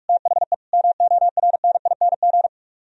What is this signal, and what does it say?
Morse code, heard as a keyed tone, about 61 wpm, spaced as for 53 wpm, 700 Hz. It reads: THE MORNING
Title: Speed-Racing - Sets of 3 Words 45wpm